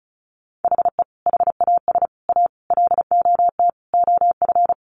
Morse code, heard as a keyed tone, about 34 wpm, 700 Hz. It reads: HE HAS A LOT OF
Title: Additional Easy Phrases 35wpm